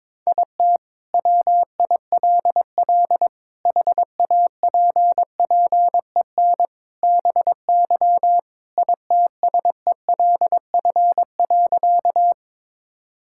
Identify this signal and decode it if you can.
Morse code, heard as a keyed tone, about 22 wpm, 700 Hz. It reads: IT WILL HAPPEN BY ITSELF.
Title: Sentences from Top 1000 Words - Review Part 5 - 22wpm